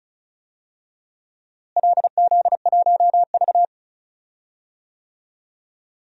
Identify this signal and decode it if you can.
Morse code, heard as a keyed tone, about 36 wpm, 700 Hz. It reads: LZ1V
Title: DXCC-100 Call Signs - 4 Characters - Repeated 2x - 35wpm